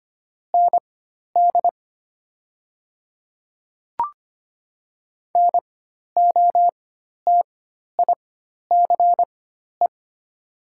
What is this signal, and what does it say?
Morse code, heard as a keyed tone, about 25 wpm, spaced as for 12 wpm, 690 Hz. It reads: ND NOTICE
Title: 4x Character Spacing - Top 500 Words - 25wpm